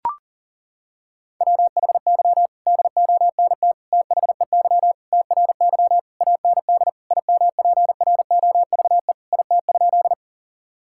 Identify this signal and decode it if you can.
Morse code, heard as a keyed tone, about 40 wpm, 700 Hz. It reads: WHY DONT THEY TRY AND IMPROVE IT?
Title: Sentences from Top 1000 Words - Review Part 5 - 40wpm